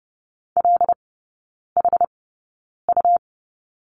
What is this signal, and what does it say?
Morse code, heard as a keyed tone, about 30 wpm, spaced as for 9 wpm, 690 Hz. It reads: LHU
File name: Lesson 17e - Letter u - Three Characters 30wpm